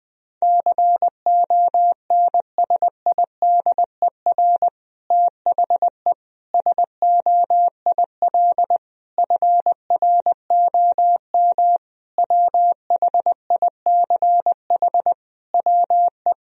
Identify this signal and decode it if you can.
Morse code, heard as a keyed tone, about 20 wpm, 700 Hz. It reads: CONSIDER THE SOIL FROM WHICH WE
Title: Sentences from Top 1000 Words - Review Part 4 - 20wpm